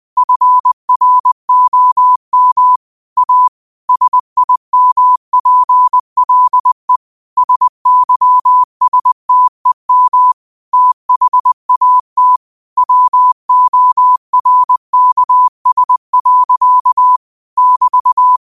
Morse code, heard as a keyed tone, about 20 wpm, 1000 Hz. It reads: FROM A SIMPLE SYSTEM THAT WORKS. <BT>
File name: qotdfiles / 20wpm